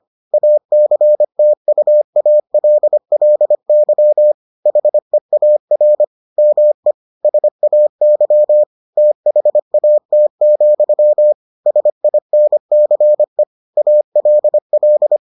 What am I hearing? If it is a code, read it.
Morse code, heard as a keyed tone, about 25 wpm, 600 Hz. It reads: ACTUALLY HEAR ME SAY THAT, SINCE ALL